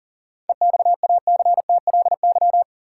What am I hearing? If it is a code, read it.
Morse code, heard as a keyed tone, about 40 wpm, 700 Hz. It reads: EXACTLY
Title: Sentences from Top 900 Words - New Word Drill - 40wpm